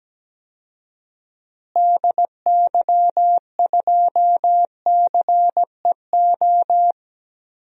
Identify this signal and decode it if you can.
Morse code, heard as a keyed tone, about 17 wpm, 700 Hz. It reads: DY2CEO
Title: DXCC-100 Call Signs - 6 Characters - Repeated 3x - 17wpm